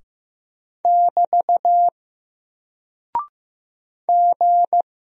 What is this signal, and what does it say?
Morse code, heard as a keyed tone, about 15 wpm, 700 Hz. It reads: <BT> G
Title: Single Character (Letter:Number:Punctuation:Prosign) 15wpm